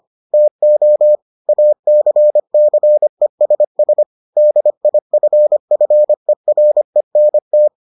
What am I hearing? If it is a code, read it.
Morse code, heard as a keyed tone, about 25 wpm, 600 Hz. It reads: TO ACCESS DIFFERENT